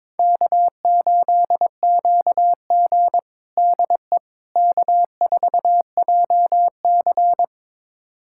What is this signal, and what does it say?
Morse code, heard as a keyed tone, about 22 wpm, 700 Hz. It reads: K8QG DE K4JC
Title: USA Callsign DE Callsign - 4 Characters - 22wpm